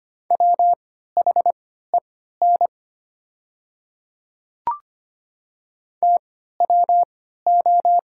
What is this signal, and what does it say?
Morse code, heard as a keyed tone, about 25 wpm, spaced as for 14 wpm, 700 Hz. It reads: WHEN TWO
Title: 3x Character Spacing - Top 100 Words - 25wpm